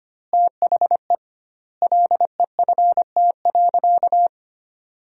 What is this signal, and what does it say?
Morse code, heard as a keyed tone, about 25 wpm, 700 Hz. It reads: THE LEFT.
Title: 2x Word-Spacing - Sentences from Top 300 Words - 25wpm